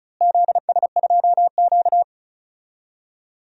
Morse code, heard as a keyed tone, about 34 wpm, 700 Hz. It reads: ZS2Q
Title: DXCC-100 Call Signs - 4 Characters - Repeated 3x - 35wpm